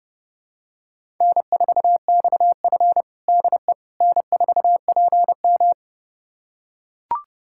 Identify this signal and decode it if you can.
Morse code, heard as a keyed tone, about 30 wpm, 700 Hz. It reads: N4XF DE N4PM
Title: USA Callsign DE Callsign - 4 Characters - 30wpm